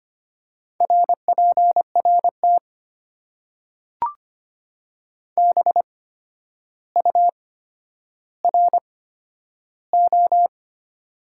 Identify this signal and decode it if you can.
Morse code, heard as a keyed tone, about 25 wpm, spaced as for 11 wpm, 690 Hz. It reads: RPRT BURO
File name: Top 90 QSO Elements - 8x Character Spacing - 25wpm